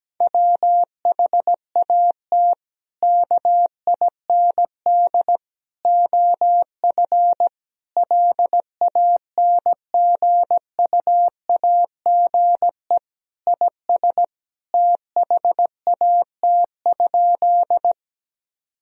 Morse code, heard as a keyed tone, about 17 wpm, 700 Hz. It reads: WHAT KIND OF LANGUAGE IS THAT?
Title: Sentences from Top 1000 Words - Review Part 3 - 17wpm